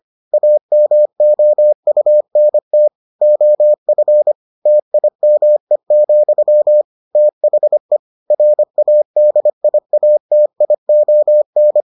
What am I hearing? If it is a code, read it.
Morse code, heard as a keyed tone, about 25 wpm, 600 Hz. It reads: AMOUNT OF TIME, THE RADIATION